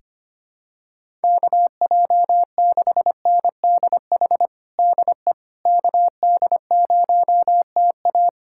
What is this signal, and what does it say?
Morse code, heard as a keyed tone, about 25 wpm, 700 Hz. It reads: KJ6NDH DE KD0TA
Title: USA Callsign DE Callsign 25wpm